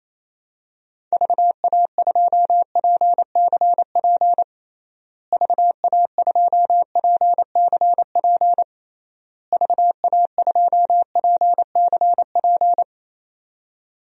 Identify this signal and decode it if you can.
Morse code, heard as a keyed tone, about 28 wpm, 700 Hz. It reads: VA2PCP VA2PCP VA2PCP
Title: DXCC-100 Call Signs - 6 Characters - Repeated 3x - 28wpm